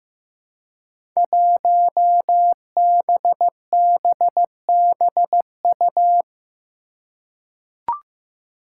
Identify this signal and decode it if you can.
Morse code, heard as a keyed tone, about 15 wpm, 700 Hz. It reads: 1BBBU
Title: Five Challenging Letter-Number Combinations 15wpm